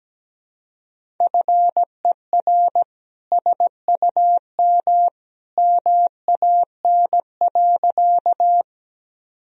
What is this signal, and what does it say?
Morse code, heard as a keyed tone, about 17 wpm, 700 Hz. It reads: FER SUM MAN.
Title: Sentences from Top 200 Words and Abbreviations 17wpm